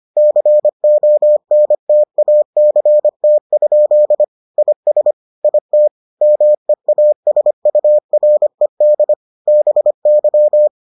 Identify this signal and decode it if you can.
Morse code, heard as a keyed tone, about 25 wpm, 600 Hz. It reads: CONTACT? IS IT MEASURED BY